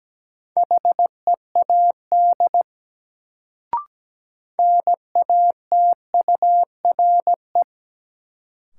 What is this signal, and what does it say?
Morse code, heard as a keyed tone, about 17 wpm, 700 Hz. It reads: HEAD NATURE
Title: Sets of 3 Words - New Word Drill - 17wpm